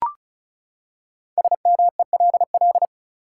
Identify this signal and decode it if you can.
Morse code, heard as a keyed tone, about 36 wpm, 700 Hz. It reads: SMELL
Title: Five Letter Words - 35wpm